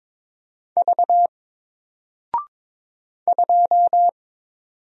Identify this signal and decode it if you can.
Morse code, heard as a keyed tone, about 22 wpm, 700 Hz. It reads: V 2
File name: Single-Letter-Number-22wpm